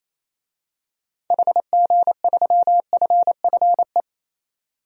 Morse code, heard as a keyed tone, about 28 wpm, 700 Hz. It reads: HG3FFE
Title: DXCC-100 Call Signs - 6 Characters - 28wpm